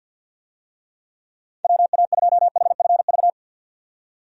Morse code, heard as a keyed tone, about 52 wpm, 700 Hz. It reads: WA2HFV
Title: USA Call Signs - 6 Characters - 50wpm